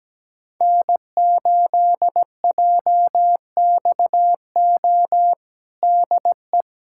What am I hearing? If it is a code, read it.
Morse code, heard as a keyed tone, about 17 wpm, 700 Hz. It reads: N8JXO DE
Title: USA Callsign DE Callsign 17wpm